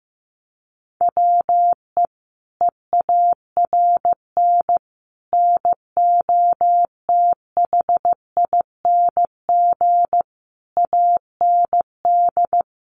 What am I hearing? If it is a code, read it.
Morse code, heard as a keyed tone, about 15 wpm, 700 Hz. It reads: WE EARN NOTHING AND
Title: Sentences from Top 1500 Words 15wpm